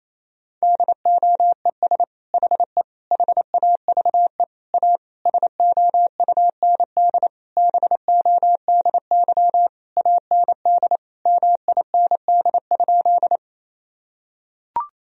Morse code, heard as a keyed tone, about 27 wpm, 700 Hz. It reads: DOES HE HAVE A SOUND BODY AND MIND?
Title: Sentences from Top 1000 Words - Review Part 2 - 28wpm